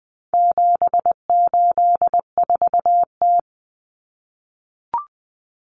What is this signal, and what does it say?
Morse code, heard as a keyed tone, about 20 wpm, 700 Hz. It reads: 784T
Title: Common Cut Numbers - 4-Digit 20wpm